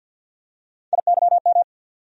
Morse code, heard as a keyed tone, about 47 wpm, 700 Hz. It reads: IYM
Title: USA Call Sign Postfixes - 3 Letters 50wpm